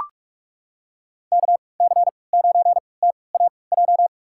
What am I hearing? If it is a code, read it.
Morse code, heard as a keyed tone, about 44 wpm, spaced as for 25 wpm, 700 Hz. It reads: KC9TAJ
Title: USA Call Signs - 6 Characters - 3x Character Spacing - 45wpm